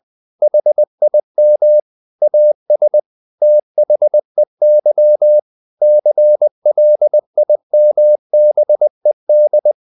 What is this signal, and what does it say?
Morse code, heard as a keyed tone, about 20 wpm, 600 Hz. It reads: HIM AS THEY CLIMBED
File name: the old man and the sea / the old man and the sea part 1 20wpm